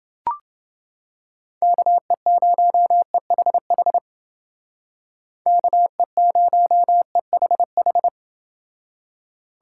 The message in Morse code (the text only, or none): KE0EHH KE0EHH